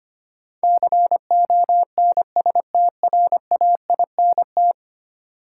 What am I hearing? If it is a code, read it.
Morse code, heard as a keyed tone, about 25 wpm, 700 Hz. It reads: CONSTRAINT
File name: 10 Letter Words - First 3 Letters Given - 25wpm